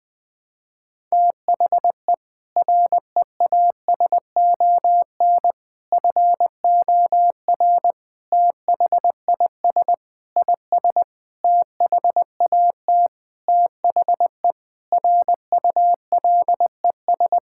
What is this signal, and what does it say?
Morse code, heard as a keyed tone, about 20 wpm, 700 Hz. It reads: THE REASON FOR THIS IS THAT THE RULES
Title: Sentences with 7-Word Phrases - 20wpm